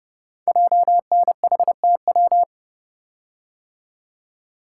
Morse code, heard as a keyed tone, about 28 wpm, 700 Hz. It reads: JNHTW
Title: Code Groups - 5 Letters - 30wpm